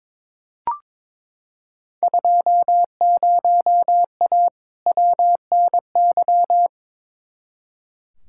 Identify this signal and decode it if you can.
Morse code, heard as a keyed tone, about 22 wpm, 700 Hz. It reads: 20A WNY
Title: Field Day Exchanges Encore - 22wpm